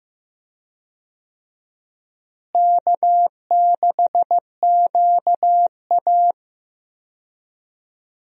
Morse code, heard as a keyed tone, about 15 wpm, 700 Hz. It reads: K6QA
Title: USA Call Signs - 4 Characters - Repeated 2x - 15wpm